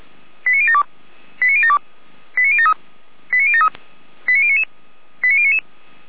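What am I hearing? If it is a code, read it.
Five-tone selective call (ZVEI), heard as quick runs of five tones, tones of about 70 ms. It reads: 89072 89072 89073 89073 89099 89099
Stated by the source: ZVEI five-tone selective call received on SDR.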